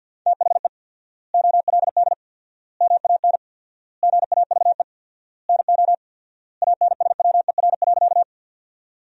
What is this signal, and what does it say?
Morse code, heard as a keyed tone, about 53 wpm, 700 Hz. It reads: THE OLD MAN GAVE NO ANSWER.
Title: 4x Word-Spacing - Sentences from Top 500 Words - 50wpm